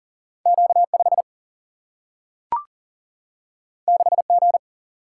Morse code, heard as a keyed tone, about 40 wpm, 700 Hz. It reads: Q 5 6 G
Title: Single-Letter Single-Number 40wpm